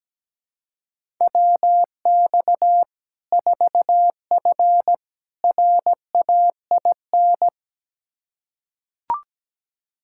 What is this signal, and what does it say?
Morse code, heard as a keyed tone, about 17 wpm, 700 Hz. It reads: WX 4F RAIN
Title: Practice QSO - Exchange 2 - Info - 17wpm